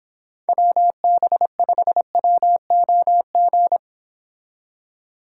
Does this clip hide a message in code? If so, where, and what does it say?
Yes: Morse code from 0.5 to 3.8 s, heard as a keyed tone, about 29 wpm, 700 Hz. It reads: WB5WOG